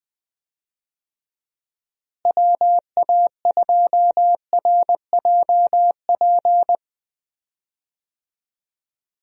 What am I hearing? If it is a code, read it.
Morse code, heard as a keyed tone, about 21 wpm, 700 Hz. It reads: WA2RJP
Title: USA Call Signs - 6 Characters - Repeated 2x - 20wpm